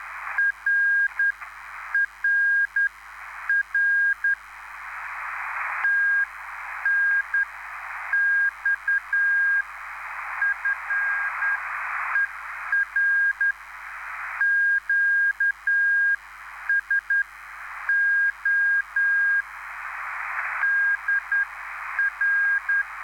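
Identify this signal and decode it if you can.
Morse code, heard as a keyed tone, about 10 wpm, spaced as for 8 wpm, 1600 Hz. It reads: RRR TNX FER QSO DR
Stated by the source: Live CW QSO between PA2BX and SP3JFK on the 14055 kHz QRS frequency, at a basic 8 wpm.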